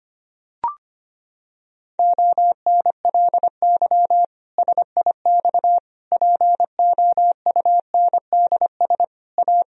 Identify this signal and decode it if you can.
Morse code, heard as a keyed tone, about 25 wpm, 700 Hz. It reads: ONLY SIX POUNDS A
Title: Sentences from Top 1000 Words - Review Part 2 - 25wpm